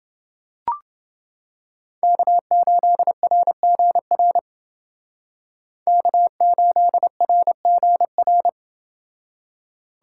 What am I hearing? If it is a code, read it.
Morse code, heard as a keyed tone, about 29 wpm, 700 Hz. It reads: K8RGR K8RGR